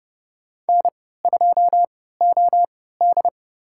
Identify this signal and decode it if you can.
Morse code, heard as a keyed tone, about 30 wpm, spaced as for 17 wpm, 700 Hz. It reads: N2OD
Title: USA Call Signs - 4 Characters - 3x Character Spacing - 30wpm